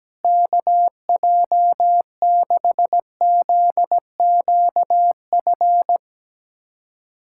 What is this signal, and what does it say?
Morse code, heard as a keyed tone, about 17 wpm, 700 Hz. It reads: KJ6ZQF
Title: USA Callsign DE Callsign 17wpm